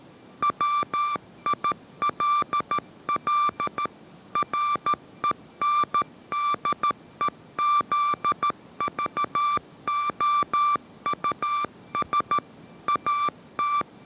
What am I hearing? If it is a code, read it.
Morse code, heard as a keyed tone, about 13 wpm, 1170 Hz. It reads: WILL RENDEZVOUS AT
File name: 14wpm